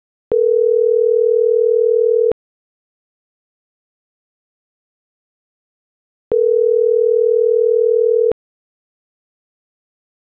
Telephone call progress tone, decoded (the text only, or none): ringback tone